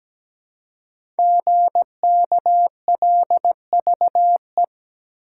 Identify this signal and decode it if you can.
Morse code, heard as a keyed tone, about 17 wpm, 700 Hz. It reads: GKLVE
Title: Code Groups - 5 Letter:Numbers - 17wpm